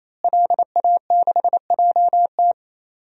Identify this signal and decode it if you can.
Morse code, heard as a keyed tone, about 28 wpm, 700 Hz. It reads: LA6JT
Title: DXCC-100 Call Signs - 5 Characters - Repeated 3x - 28wpm